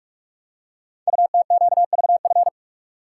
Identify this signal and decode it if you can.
Morse code, heard as a keyed tone, about 44 wpm, 700 Hz. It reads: UTQVF
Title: Code Groups - 5 Letters - 45wpm